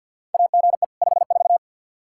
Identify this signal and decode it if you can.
Morse code, heard as a keyed tone, about 50 wpm, 700 Hz. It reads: AGE 54
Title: K1USN SST QSO Practice 50wpm